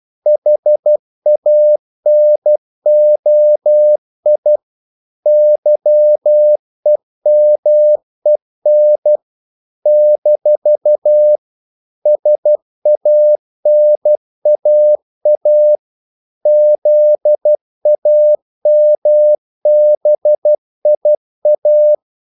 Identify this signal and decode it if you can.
Morse code, heard as a keyed tone, about 12 wpm, 600 Hz. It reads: HANOI YEMEN - SANAA ZAMBIA
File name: CW / Practice / NationalCapitals12